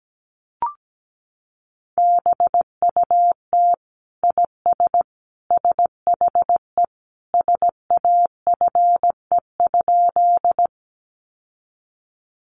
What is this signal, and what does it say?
Morse code, heard as a keyed tone, about 17 wpm, 700 Hz. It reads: BUT IS SHE SAFE?
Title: Sentences from Top 900 Words 17wpm